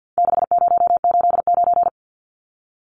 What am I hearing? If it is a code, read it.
Morse code, heard as a keyed tone, about 50 wpm, 700 Hz. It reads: 6089
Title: Numbers - 4-Digit 50wpm